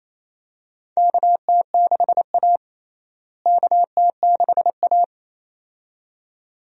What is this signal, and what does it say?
Morse code, heard as a keyed tone, about 28 wpm, 700 Hz. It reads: KT6A KT6A
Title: USA Call Signs - 4 Characters - Repeated 2x - 28wpm